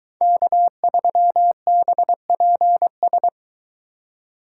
Morse code, heard as a keyed tone, about 22 wpm, 700 Hz. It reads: K3BPS